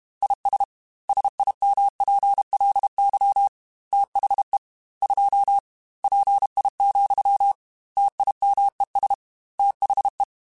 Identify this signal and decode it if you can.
Morse code, heard as a keyed tone, about 32 wpm, 800 Hz. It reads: IS SIMPLY THE 2 PI, TIMES THE
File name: Technical Subjects / Technical Subjects-mp3s / XL @30